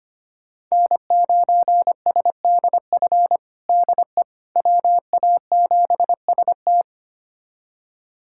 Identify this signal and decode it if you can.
Morse code, heard as a keyed tone, about 25 wpm, 700 Hz. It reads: N9SDF DE WA7ST
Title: USA Callsign DE Callsign - 5 Characters - 25wpm